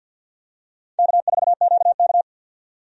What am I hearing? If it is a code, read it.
Morse code, heard as a keyed tone, about 53 wpm, 700 Hz. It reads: K4QK
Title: USA Call Signs - 4 Characters - 50wpm